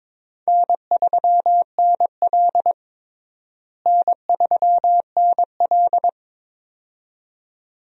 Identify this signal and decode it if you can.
Morse code, heard as a keyed tone, about 21 wpm, 700 Hz. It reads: N3NL N3NL
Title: USA Call Signs - 4 Characters - Repeated 2x - 22wpm